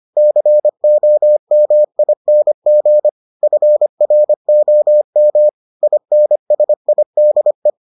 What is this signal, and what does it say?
Morse code, heard as a keyed tone, about 25 wpm, 600 Hz. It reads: COMING FROM INSIDE